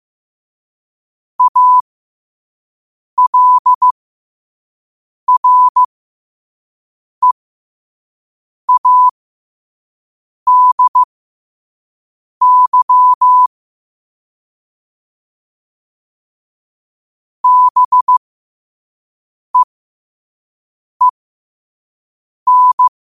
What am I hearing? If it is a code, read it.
Morse code, heard as a keyed tone, about 15 wpm, spaced as for 5 wpm, 1000 Hz. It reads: ALREADY BEEN
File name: qotdfiles / 5wpm